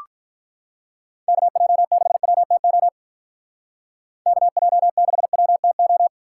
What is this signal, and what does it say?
Morse code, heard as a keyed tone, about 44 wpm, 700 Hz. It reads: KJ6WTO KJ6WTO